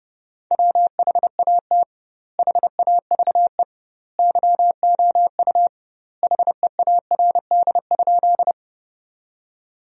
Morse code, heard as a keyed tone, about 30 wpm, 700 Hz. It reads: WHAT HAVE YOU HEARD?
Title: 2x Word-Spacing - Sentences from Top 400 Words - 30wpm